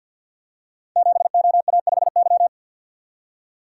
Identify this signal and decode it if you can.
Morse code, heard as a keyed tone, about 52 wpm, 700 Hz. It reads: 7OA5Y
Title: Code Groups - 5 Letter:Numbers - 50wpm